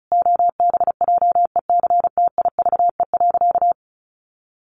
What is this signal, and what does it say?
Morse code, heard as a keyed tone, about 34 wpm, 700 Hz. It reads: OBJECTIVE.
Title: Sentences from Top 1200 Words 35wpm